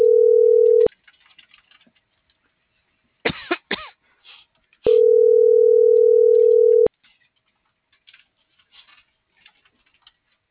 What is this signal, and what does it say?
Ringback tone (North America), 440 + 480 Hz.
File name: CENTRAL MORENA / out dulce.morena.2024